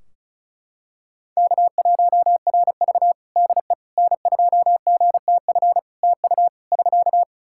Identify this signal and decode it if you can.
Morse code, heard as a keyed tone, about 34 wpm, 700 Hz. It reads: K1RV DE N2GTF TU <SK>
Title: Practice QSO - Exchange 4 - Ending - 35wpm